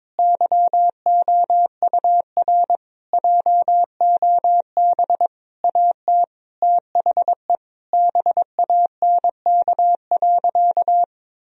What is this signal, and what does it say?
Morse code, heard as a keyed tone, about 21 wpm, 700 Hz. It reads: YOUR JOB AT THE BANK.
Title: Sentences from Top 1000 Words - Review Part 4 - 22wpm